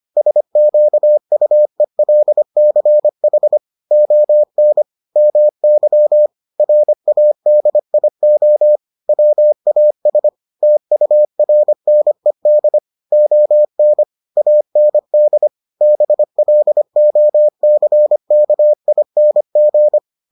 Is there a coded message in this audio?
Yes: Morse code from 0.2 to 20.0 s, heard as a keyed tone, about 25 wpm, 600 Hz. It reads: SQUELCH ON MY RADIO WAS TURNED ON AND BLOCKING